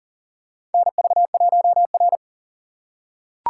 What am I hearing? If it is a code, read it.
Morse code, heard as a keyed tone, about 40 wpm, 700 Hz. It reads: NV1R
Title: USA Call Signs - 4 Characters - 6x Character Spacing - 40wpm